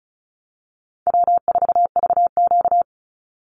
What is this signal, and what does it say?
Morse code, heard as a keyed tone, about 34 wpm, 700 Hz. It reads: W4VQ
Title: Easy Call Signs Repeated 3x - 35wpm